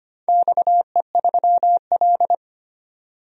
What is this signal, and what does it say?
Morse code, heard as a keyed tone, about 25 wpm, 700 Hz. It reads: XE3L
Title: DXCC-100 Call Signs - 4 Characters - 25wpm